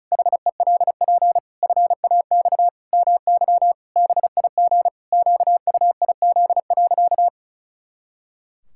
Morse code, heard as a keyed tone, about 34 wpm, 700 Hz. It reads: HELP FAX MY BIG QUIZ.
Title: Pangram Challenge v2 - 35wpm